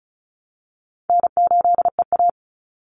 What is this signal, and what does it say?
Morse code, heard as a keyed tone, about 34 wpm, 700 Hz. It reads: N8EA
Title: Call Signs Repeated 3x - 35wpm